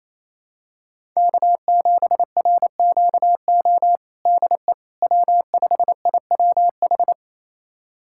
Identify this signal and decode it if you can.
Morse code, heard as a keyed tone, about 28 wpm, 700 Hz. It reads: K7RQO DE W5IWH
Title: USA Callsign DE Callsign - 5 Characters - 28wpm